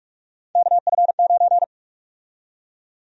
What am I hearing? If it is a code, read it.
Morse code, heard as a keyed tone, about 46 wpm, 700 Hz. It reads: KF9
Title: USA Call Sign Prefixes - 45wpm